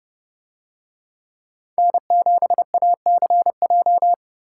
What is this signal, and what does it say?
Morse code, heard as a keyed tone, about 30 wpm, 700 Hz. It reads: N7ACJ